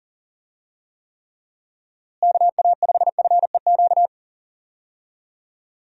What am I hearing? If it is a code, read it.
Morse code, heard as a keyed tone, about 38 wpm, 700 Hz. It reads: KA5FEQ
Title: USA Call Signs - 6 Characters - Repeated 3x - 40wpm